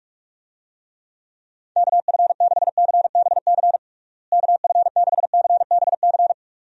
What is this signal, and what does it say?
Morse code, heard as a keyed tone, about 46 wpm, 700 Hz. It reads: KF6CBC KF6CBC
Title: USA Call Signs - 6 Characters - Repeated 2x - 45wpm